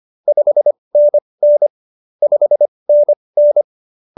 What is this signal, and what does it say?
Morse code, heard as a keyed tone, about 25 wpm, spaced as for 20 wpm, 590 Hz. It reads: 5NN 5NN
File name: QSO309 20